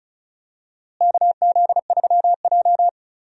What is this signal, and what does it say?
Morse code, heard as a keyed tone, about 36 wpm, 700 Hz. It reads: KZ3J
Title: USA Call Signs - 4 Characters - Repeated 3x - 35wpm